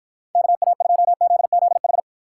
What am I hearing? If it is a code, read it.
Morse code, heard as a keyed tone, about 49 wpm, 700 Hz. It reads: KA2ZZH